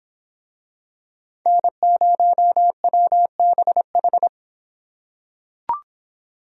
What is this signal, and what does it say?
Morse code, heard as a keyed tone, about 25 wpm, 700 Hz. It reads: N0WBH